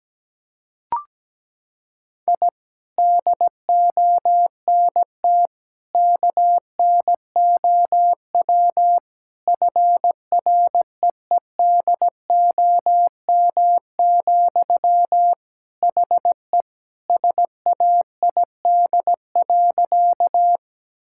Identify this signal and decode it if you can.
Morse code, heard as a keyed tone, about 17 wpm, 700 Hz. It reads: I DONT KNOW FREEDOM, HE SAID.
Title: Sentences from Top 1000 Words - Review Part 5 - 17wpm